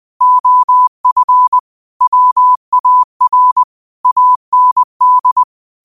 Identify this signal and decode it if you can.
Morse code, heard as a keyed tone, about 20 wpm, 1000 Hz. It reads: OF WAR AND